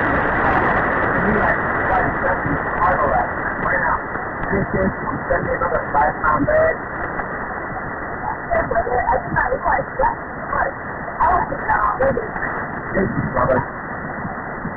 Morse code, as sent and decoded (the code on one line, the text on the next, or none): none